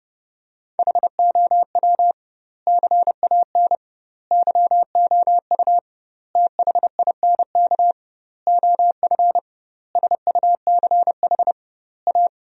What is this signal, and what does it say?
Morse code, heard as a keyed tone, about 30 wpm, 700 Hz. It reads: HOW CAN YOU THINK OF SUCH A
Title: 2x Word-Spacing - Sentences from Top 200 Words - 30wpm